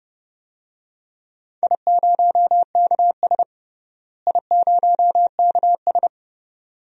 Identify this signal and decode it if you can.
Morse code, heard as a keyed tone, about 31 wpm, 700 Hz. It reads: I0KS I0KS
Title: DXCC-100 Call Signs - 4 Characters - Repeated 2x - 30wpm